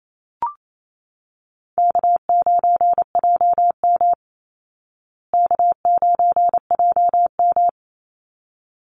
Morse code, heard as a keyed tone, about 28 wpm, 700 Hz. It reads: K9JM K9JM
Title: Easy Call Signs Repeated 2x - 28wpm